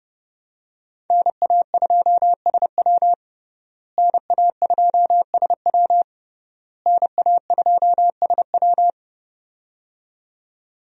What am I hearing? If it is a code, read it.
Morse code, heard as a keyed tone, about 30 wpm, 700 Hz. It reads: NA2SW NA2SW NA2SW
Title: USA Call Signs - 5 Characters - Repeated 3x - 30wpm